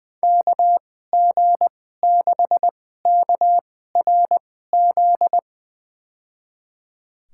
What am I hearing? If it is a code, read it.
Morse code, heard as a keyed tone, about 20 wpm, spaced as for 15 wpm, 700 Hz. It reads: KG6KRZ
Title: USA Call Signs - 6 Characters - 2x Character Spacing - 20wpm